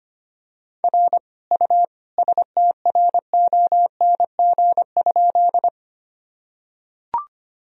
Morse code, heard as a keyed tone, about 25 wpm, 700 Hz. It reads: R U STRONG?
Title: Sentences from Top 500 Words and Abbreviations 25wpm